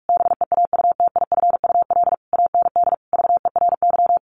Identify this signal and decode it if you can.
Morse code, heard as a keyed tone, about 44 wpm, 700 Hz. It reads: BEAUTIFUL AND VERY
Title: A Sound Of Thunder 45wpm